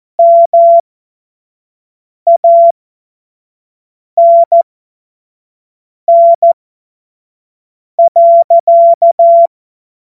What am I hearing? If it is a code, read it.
Morse code, heard as a keyed tone, about 14 wpm, spaced as for 5 wpm, 680 Hz. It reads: MANN.
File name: Morsekurs / Witze / Witz 05 Dezember 2023 25 F70